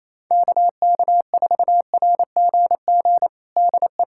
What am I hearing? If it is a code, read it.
Morse code, heard as a keyed tone, about 28 wpm, 700 Hz. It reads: KK4RGG DE
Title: USA Callsign DE Callsign 28wpm